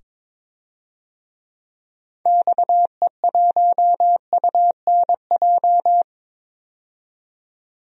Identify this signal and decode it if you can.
Morse code, heard as a keyed tone, about 22 wpm, 700 Hz. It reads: XE1UNJ
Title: DXCC-100 Call Signs - 6 Characters - 22wpm